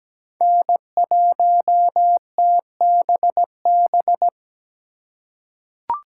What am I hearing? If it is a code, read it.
Morse code, heard as a keyed tone, about 17 wpm, 700 Hz. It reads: N1TBB
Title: USA Call Signs - 4x Character Spacing - 17wpm